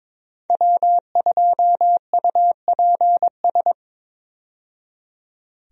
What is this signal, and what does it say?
Morse code, heard as a keyed tone, about 23 wpm, 700 Hz. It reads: W2UPS
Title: USA Call Signs - 5 Characters - 22wpm